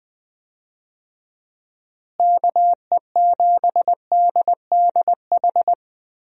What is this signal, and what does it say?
Morse code, heard as a keyed tone, about 20 wpm, 700 Hz. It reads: KE7DDH
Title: USA Call Signs - 6 Characters - Repeated 3x - 20wpm